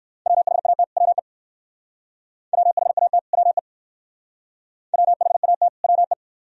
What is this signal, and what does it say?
Morse code, heard as a keyed tone, about 58 wpm, 690 Hz. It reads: WHAT WE WHAT WE WHAT WE